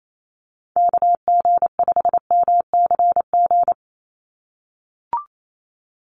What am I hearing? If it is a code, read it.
Morse code, heard as a keyed tone, about 28 wpm, 700 Hz. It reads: KG5MCG
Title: USA Call Signs 28wpm